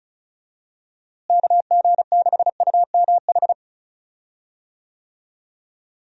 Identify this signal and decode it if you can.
Morse code, heard as a keyed tone, about 36 wpm, 700 Hz. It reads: KG6UMH
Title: USA Call Signs - 6 Characters - Repeated 2x - 35wpm